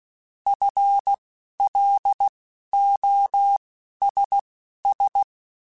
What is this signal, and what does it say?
Morse code, heard as a keyed tone, about 16 wpm, spaced as for 11 wpm, 800 Hz. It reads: FLOSS
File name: Alphabetical / Alphabetical mp3s / Fs @10